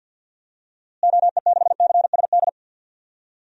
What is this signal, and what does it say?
Morse code, heard as a keyed tone, about 49 wpm, 700 Hz. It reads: OE6CSD
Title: DXCC-100 Call Signs - 6 Characters - Repeated 2x - 50wpm